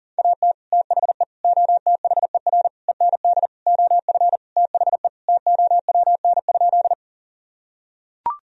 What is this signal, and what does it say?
Morse code, heard as a keyed tone, about 40 wpm, 700 Hz. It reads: AT THE OTHER END OF THE TOWN?
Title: Sentences from Top 1000 Words - Review Part 2 - 40wpm